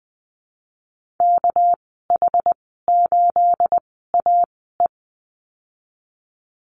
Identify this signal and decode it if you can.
Morse code, heard as a keyed tone, about 20 wpm, spaced as for 15 wpm, 700 Hz. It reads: KH8AE
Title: 219 Lesson 35g - Number 0 - Call Signs 20wpm